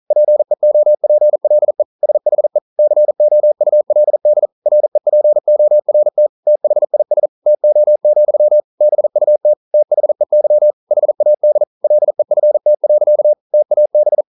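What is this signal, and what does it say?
Morse code, heard as a keyed tone, about 41 wpm, 600 Hz. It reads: PEOPLE SHE COULD REPORT THIS TO, BUT THEY HAD LEFT. TAB
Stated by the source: Morse code practice at 41 wpm, from CW med Gnister.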